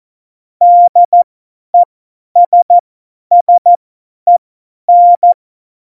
Morse code, heard as a keyed tone, about 14 wpm, spaced as for 10 wpm, 700 Hz. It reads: DESSEN
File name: Morsekurs / Witze / Witz 29. September 2020, 50 F70